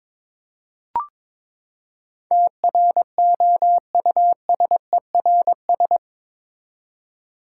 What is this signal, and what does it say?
Morse code, heard as a keyed tone, about 22 wpm, 700 Hz. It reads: TROUSERS
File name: practice / 22wpm / Eight Letter Words - 22wpm